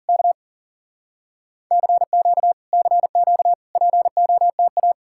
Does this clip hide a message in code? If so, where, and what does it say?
Yes: Morse code from 0.1 to 4.9 s, heard as a keyed tone, about 52 wpm, 700 Hz. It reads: K CQ CQ POTA